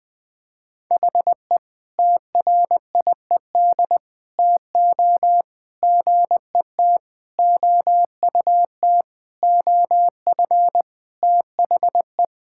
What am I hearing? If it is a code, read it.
Morse code, heard as a keyed tone, about 20 wpm, 700 Hz. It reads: HE TRIED TO GET OUT OF THE
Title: Sentences with 5-Word Phrases - 20wpm